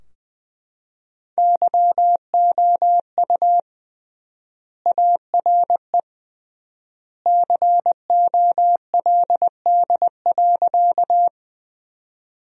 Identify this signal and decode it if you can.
Morse code, heard as a keyed tone, about 20 wpm, 700 Hz. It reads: YOU ARE COLD.
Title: 3x Word-Spacing - Sentences from Top 400 Words - 20wpm